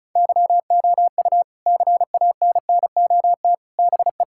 Morse code, heard as a keyed tone, about 34 wpm, 700 Hz. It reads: YOU CANNOT BE
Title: Sentences from Top 1000 Words - Review Part 6 - 35wpm